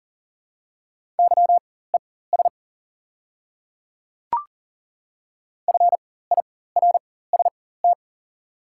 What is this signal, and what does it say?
Morse code, heard as a keyed tone, about 38 wpm, spaced as for 18 wpm, 700 Hz. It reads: YES FIRST
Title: 4x Character Spacing - Top 500 Words - 40wpm